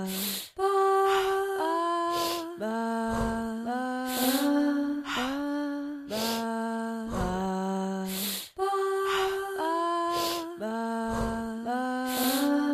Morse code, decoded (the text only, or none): none